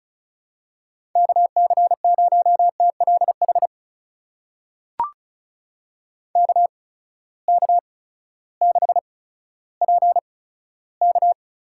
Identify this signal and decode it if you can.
Morse code, heard as a keyed tone, about 35 wpm, spaced as for 16 wpm, 700 Hz. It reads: KC0TLH KK6PK
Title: USA Call Signs - 6 Characters - 8x Character Spacing - 35wpm